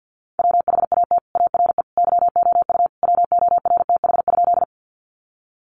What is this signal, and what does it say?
Morse code, heard as a keyed tone, about 50 wpm, 700 Hz. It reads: WHAT ARE YOU WORTH?
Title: Sentences from Top 800 Words 50wpm